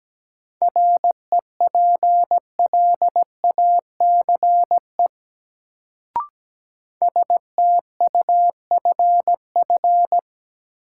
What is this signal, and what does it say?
Morse code, heard as a keyed tone, about 17 wpm, 700 Hz. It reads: REPLACE STUFF
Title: Sentences from Top 1400 Words - New Word Drill Alt - 17wpm